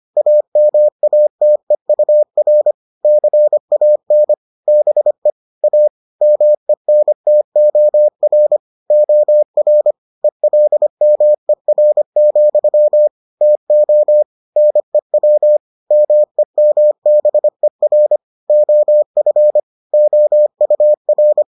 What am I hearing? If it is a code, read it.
Morse code, heard as a keyed tone, about 25 wpm, 600 Hz. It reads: AMATEUR CAN BE A MENTOR OR ELMER, TO NEW MEMBER OF OUR